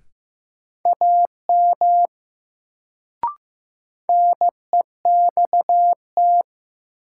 Morse code, heard as a keyed tone, about 15 wpm, 700 Hz. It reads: AM NEXT
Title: Sets of 2 Words - New Word Drill - 15wpm